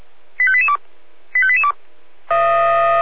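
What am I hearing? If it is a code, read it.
Five-tone selective call (ZVEI), heard as quick runs of five tones, tones of about 70 ms. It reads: 85882 85882
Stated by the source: Received on an SDR, ZVEI five-tone code 85354.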